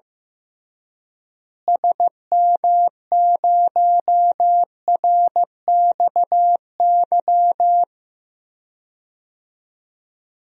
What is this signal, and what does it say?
Morse code, heard as a keyed tone, about 15 wpm, 700 Hz. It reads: SM0RXY
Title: DXCC-100 Call Signs - 6 Characters - Repeated 3x - 15wpm